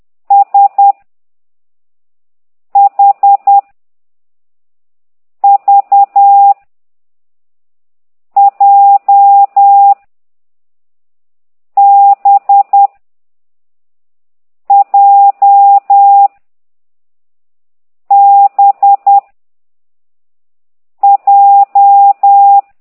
Morse code, heard as a keyed tone, about 10 wpm, 800 Hz. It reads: S H V J B J B J
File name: ct2idw / exerc13